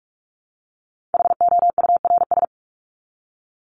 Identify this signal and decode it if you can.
Morse code, heard as a keyed tone, about 44 wpm, 700 Hz. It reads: HOURS
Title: ES:S Suffix - Top 100 Words with Base Words 45wpm